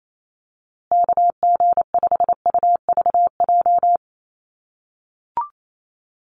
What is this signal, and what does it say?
Morse code, heard as a keyed tone, about 28 wpm, 700 Hz. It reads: KG5UVJ
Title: USA Call Signs Repeated 3x 28wpm